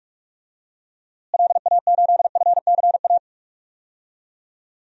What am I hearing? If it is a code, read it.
Morse code, heard as a keyed tone, about 44 wpm, 700 Hz. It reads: LA8FCA
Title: DXCC-100 Call Signs - 6 Characters - Repeated 2x - 45wpm